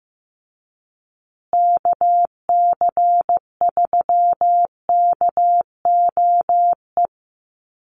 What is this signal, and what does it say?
Morse code, heard as a keyed tone, about 15 wpm, 700 Hz. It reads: KC3KOE
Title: USA Call Signs Repeated 3x 15wpm